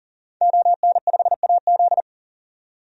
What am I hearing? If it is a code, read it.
Morse code, heard as a keyed tone, about 40 wpm, 700 Hz. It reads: ON5AZ
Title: DXCC-100 Call Signs - 5 Characters - 40wpm